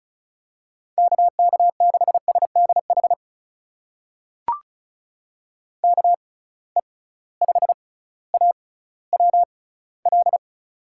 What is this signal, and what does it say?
Morse code, heard as a keyed tone, about 36 wpm, spaced as for 18 wpm, 700 Hz. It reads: KK6SDH KE5AWL
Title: USA Call Signs - 6 Characters - 6x Character Spacing - 35wpm